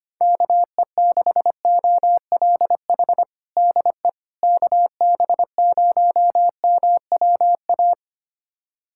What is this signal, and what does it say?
Morse code, heard as a keyed tone, about 25 wpm, 700 Hz. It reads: KE6OLH DE KB0MWA
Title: USA Callsign DE Callsign - 6 Characters - 25wpm